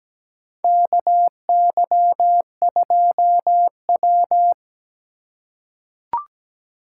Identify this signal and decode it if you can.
Morse code, heard as a keyed tone, about 17 wpm, 700 Hz. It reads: KY2W
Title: USA Call Signs - 4 Characters - 2x Character Spacing - 17wpm